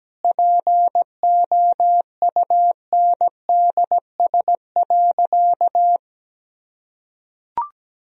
Morse code, heard as a keyed tone, about 17 wpm, 700 Hz. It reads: POUNDS.
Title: Sentences from Top 1000 Words - Review Part 2 - 17wpm